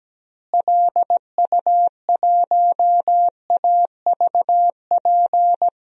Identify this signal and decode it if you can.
Morse code, heard as a keyed tone, about 17 wpm, 700 Hz. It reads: LU1AVP
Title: DXCC-100 Call Signs - 6 Characters - Repeated 3x - 17wpm